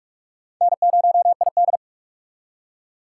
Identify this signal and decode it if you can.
Morse code, heard as a keyed tone, about 43 wpm, 700 Hz. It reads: N0ID
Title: USA Call Signs - 4x Character Spacing - 45wpm